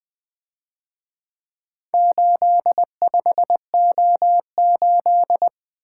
Morse code, heard as a keyed tone, about 24 wpm, 700 Hz. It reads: 85O8
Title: Speed-Racing - Four Letter-Number Combinations - 20wpm